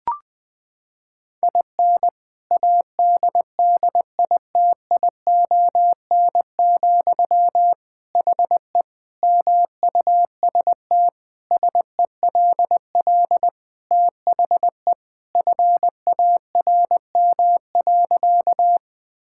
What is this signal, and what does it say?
Morse code, heard as a keyed tone, about 21 wpm, 700 Hz. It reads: IN ADDITION, HE MUST SELL THE FARM.
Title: Sentences from Top 1000 Words - Review Part 6 - 20wpm